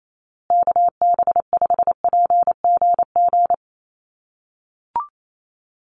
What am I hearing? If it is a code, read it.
Morse code, heard as a keyed tone, about 28 wpm, 700 Hz. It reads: KB5PGG
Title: USA Call Signs Repeated 2x 28wpm